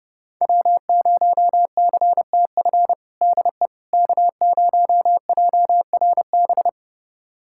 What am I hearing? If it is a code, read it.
Morse code, heard as a keyed tone, about 30 wpm, 700 Hz. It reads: W0CTF DE K0JRB
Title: USA Callsign DE Callsign - 5 Characters - 30wpm